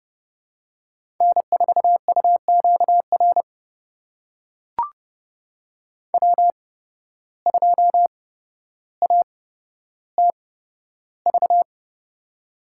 Morse code, heard as a keyed tone, about 30 wpm, spaced as for 13 wpm, 700 Hz. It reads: N4UQR W2ATV
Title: USA Call Signs - 5 Characters - 8x Character Spacing - 30wpm